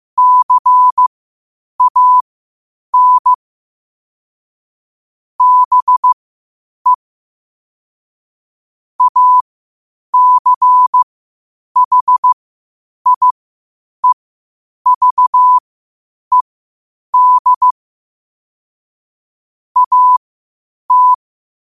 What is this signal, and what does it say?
Morse code, heard as a keyed tone, about 15 wpm, spaced as for 8 wpm, 1000 Hz. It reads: CAN BE ACHIEVED AT
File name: qotdfiles / 8wpm